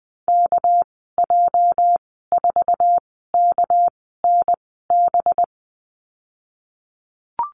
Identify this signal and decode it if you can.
Morse code, heard as a keyed tone, about 20 wpm, spaced as for 15 wpm, 700 Hz. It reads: KJ4KNB
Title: 207 Lesson 33g - Letter j - Call Signs 20wpm